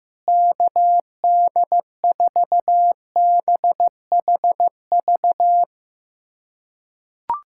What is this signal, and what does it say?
Morse code, heard as a keyed tone, about 15 wpm, 700 Hz. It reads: KD4BHV
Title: USA Callsign DE Callsign 15wpm